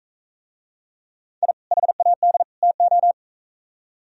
Morse code, heard as a keyed tone, about 45 wpm, 700 Hz. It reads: I HAD TO